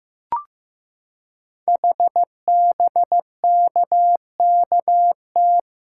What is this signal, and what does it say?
Morse code, heard as a keyed tone, about 15 wpm, 700 Hz. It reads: HBKKT
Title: Code Groups - 5 Letters - 15wpm